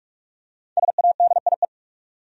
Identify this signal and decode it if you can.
Morse code, heard as a keyed tone, about 41 wpm, 700 Hz. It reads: SADIE
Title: USA Common Names - Max 5 Letters - 45wpm